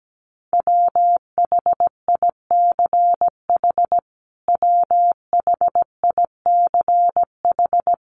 Morse code, heard as a keyed tone, about 17 wpm, 700 Hz. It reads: WHICH WHICH
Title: Top 500 Words Repeated 17wpm